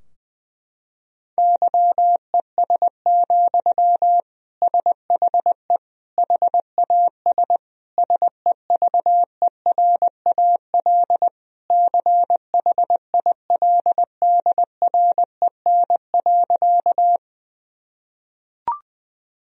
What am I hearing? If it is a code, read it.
Morse code, heard as a keyed tone, about 21 wpm, 700 Hz. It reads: YES, SHE HAS SEVERAL CHILDREN.
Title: Sentences from Top 1000 Words - Review Part 2 - 20wpm